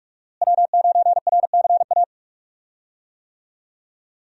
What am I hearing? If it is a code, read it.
Morse code, heard as a keyed tone, about 44 wpm, 700 Hz. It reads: W9RCA
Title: USA Call Signs - 5 Characters - Repeated 2x - 45wpm